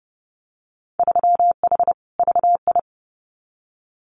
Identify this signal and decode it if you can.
Morse code, heard as a keyed tone, about 30 wpm, 700 Hz. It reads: 3H VI
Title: Winter Field Day Exchanges Encore 30wpm